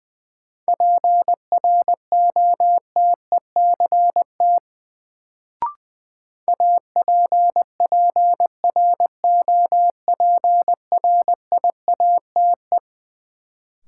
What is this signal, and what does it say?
Morse code, heard as a keyed tone, about 21 wpm, 700 Hz. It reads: PROTECT APPROPRIATE
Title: Sentences from Top 900 Words - New Word Drill - 20wpm